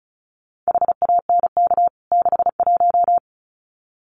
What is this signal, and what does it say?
Morse code, heard as a keyed tone, about 34 wpm, 700 Hz. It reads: HANK 61
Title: CWT Exchange 35wpm